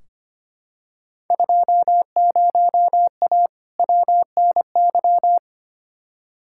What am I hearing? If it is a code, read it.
Morse code, heard as a keyed tone, about 25 wpm, 700 Hz. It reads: 20A WNY
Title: Field Day Exchanges Encore - 25wpm